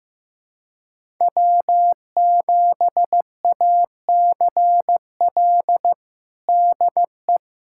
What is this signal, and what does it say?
Morse code, heard as a keyed tone, about 15 wpm, 700 Hz. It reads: W7ACL DE
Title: USA Callsign DE Callsign - 5 Characters - 15wpm